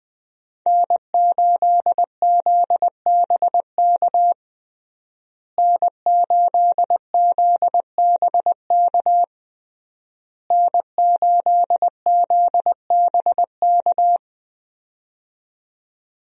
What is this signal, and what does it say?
Morse code, heard as a keyed tone, about 20 wpm, 700 Hz. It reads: N8ZBK N8ZBK N8ZBK
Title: USA Call Signs - 5 Characters - Repeated 3x - 20wpm